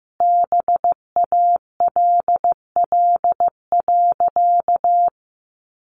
Morse code, heard as a keyed tone, about 15 wpm, 700 Hz. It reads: BALL.
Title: Sentences from Top 900 Words 15wpm